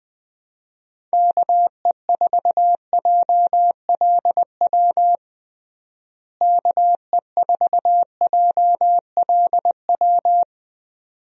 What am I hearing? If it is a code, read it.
Morse code, heard as a keyed tone, about 21 wpm, 700 Hz. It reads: KE4JLW KE4JLW
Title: USA Call Signs - 6 Characters - Repeated 2x - 20wpm